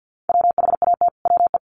Morse code, heard as a keyed tone, about 50 wpm, 700 Hz. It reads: WHAT WE
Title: Sets of 5 Words Encore 50wpm